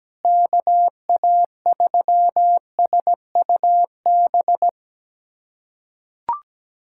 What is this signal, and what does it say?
Morse code, heard as a keyed tone, about 17 wpm, 700 Hz. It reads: KA3SUB
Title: USA Callsign DE Callsign - 6 Characters - 17wpm